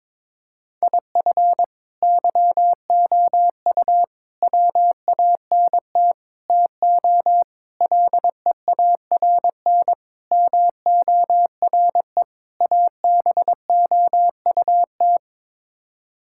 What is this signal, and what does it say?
Morse code, heard as a keyed tone, about 22 wpm, 700 Hz. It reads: IF YOU WANT TO LEARN MORE ABOUT
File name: practice / 22wpm / Sets of 7 Words - 22wpm